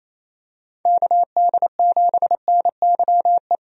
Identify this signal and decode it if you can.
Morse code, heard as a keyed tone, about 27 wpm, 700 Hz. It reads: KD7NYE
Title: USA Call Signs - 6 Characters - 28wpm